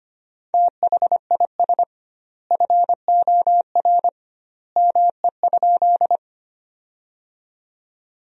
Morse code, heard as a keyed tone, about 25 wpm, 700 Hz. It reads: THIS FOR ME?
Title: 2x Word-Spacing - Sentences from Top 200 Words - 25wpm